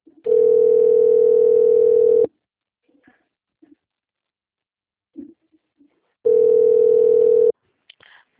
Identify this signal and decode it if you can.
Ringback tone (North America), 440 + 480 Hz.